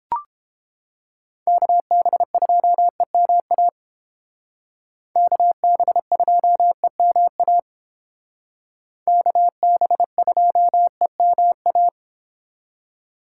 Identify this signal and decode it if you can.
Morse code, heard as a keyed tone, about 29 wpm, 700 Hz. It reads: KB2EMA KB2EMA KB2EMA